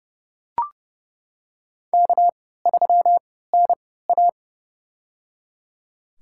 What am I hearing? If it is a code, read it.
Morse code, heard as a keyed tone, about 30 wpm, spaced as for 17 wpm, 700 Hz. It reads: K3NA
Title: USA Call Signs - 4 Characters - 3x Character Spacing - 30wpm